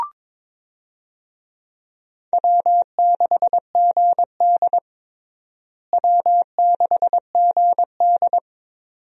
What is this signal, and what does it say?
Morse code, heard as a keyed tone, about 22 wpm, 700 Hz. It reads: W6GD W6GD
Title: USA Call Signs - 4 Characters - Repeated 2x - 22wpm